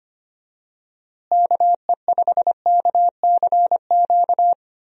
Morse code, heard as a keyed tone, about 25 wpm, 700 Hz. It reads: KE5KCQ
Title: USA Call Signs - 6 Characters - Repeated 2x - 25wpm